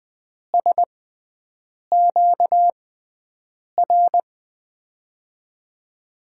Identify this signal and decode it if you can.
Morse code, heard as a keyed tone, about 20 wpm, spaced as for 7 wpm, 700 Hz. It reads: SQR
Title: USA Call Signs - 6 Characters - 6x Character Spacing - 20wpm